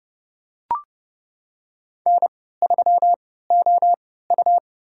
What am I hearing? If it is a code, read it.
Morse code, heard as a keyed tone, about 31 wpm, spaced as for 17 wpm, 700 Hz. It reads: N3OU
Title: USA Call Signs - 4 Characters - 3x Character Spacing - 30wpm